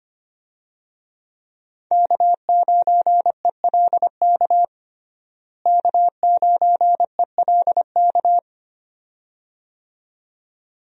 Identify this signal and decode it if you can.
Morse code, heard as a keyed tone, about 25 wpm, 700 Hz. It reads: K9ELK K9ELK
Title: USA Call Signs - 5 Characters - Repeated 2x - 25wpm